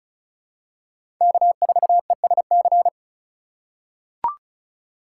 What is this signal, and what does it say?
Morse code, heard as a keyed tone, about 34 wpm, 700 Hz. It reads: K4ESC
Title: USA Call Signs - 5 Characters - 4x Character Spacing - 35wpm